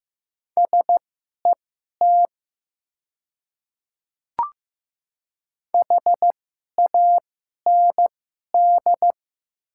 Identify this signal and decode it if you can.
Morse code, heard as a keyed tone, about 15 wpm, spaced as for 11 wpm, 690 Hz. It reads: SET HAND
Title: 2x Character Spacing - Top 200 Words - 15wpm